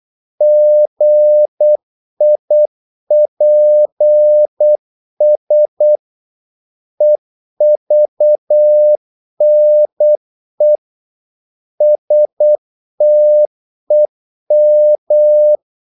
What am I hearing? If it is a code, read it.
Morse code, heard as a keyed tone, about 8 wpm, 600 Hz. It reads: GIPS EVNE STEM
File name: Korte ord DK 8wpm